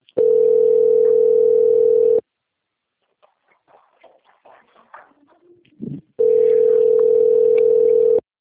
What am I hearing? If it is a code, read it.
Ringback tone (North America), 440 + 480 Hz.
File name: out- - - - - .49358